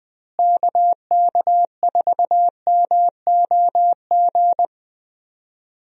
Morse code, heard as a keyed tone, about 21 wpm, 700 Hz. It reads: KK4MOG
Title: USA Call Signs - 6 Characters - Repeated 3x - 20wpm